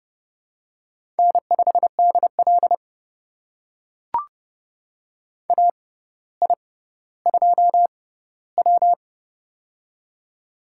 Morse code, heard as a keyed tone, about 30 wpm, spaced as for 15 wpm, 700 Hz. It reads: N5DL AI2W
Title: USA Call Signs - 4 Characters - 6x Character Spacing - 30wpm